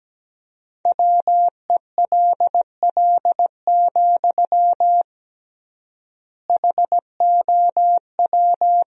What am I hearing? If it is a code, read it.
Morse code, heard as a keyed tone, about 17 wpm, 700 Hz. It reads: WELL, HOW
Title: 3x Word-Spacing - Sentences from Top 200 Words - 17wpm